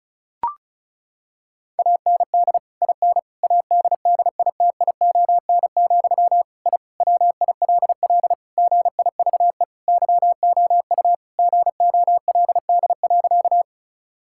Morse code, heard as a keyed tone, about 34 wpm, 700 Hz. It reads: AND IN ADDITION, I WILL GIVE YOU GOLD.
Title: Sentences from Top 1000 Words - Review Part 6 - 35wpm